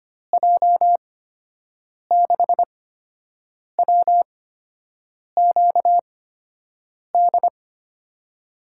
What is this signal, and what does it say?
Morse code, heard as a keyed tone, about 25 wpm, spaced as for 7 wpm, 700 Hz. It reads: J6WQD
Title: USA Call Signs - 6 Characters - 8x Character Spacing - 25wpm